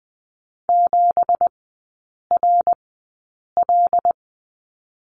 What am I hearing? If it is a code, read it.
Morse code, heard as a keyed tone, about 20 wpm, spaced as for 8 wpm, 700 Hz. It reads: 7RL
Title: 198 Lesson 32e - Letter k - Three Characters 20wpm